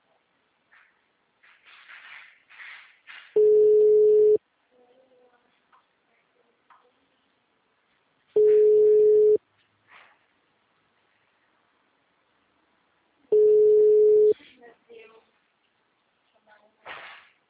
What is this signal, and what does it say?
Ringback tone (Europe, 425 Hz), with speech.